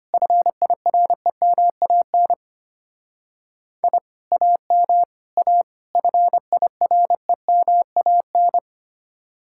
Morse code, heard as a keyed tone, about 32 wpm, 700 Hz. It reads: FIREMAN I AM A FIREMAN